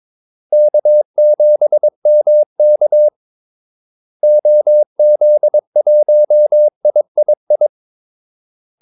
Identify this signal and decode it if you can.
Morse code, heard as a keyed tone, about 22 wpm, 600 Hz. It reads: K7MK OZ1III
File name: Callsigns-22wpm